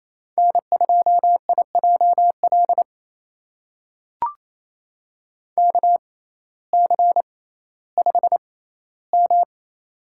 Morse code, heard as a keyed tone, about 28 wpm, spaced as for 15 wpm, 700 Hz. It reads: N2IJL KC5M
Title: USA Call Signs - 6x Character Spacing - 28wpm